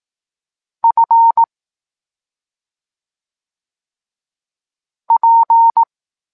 Morse code, heard as a keyed tone, about 18 wpm, 920 Hz. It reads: F P
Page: Tekst 25 – Nauka liter P F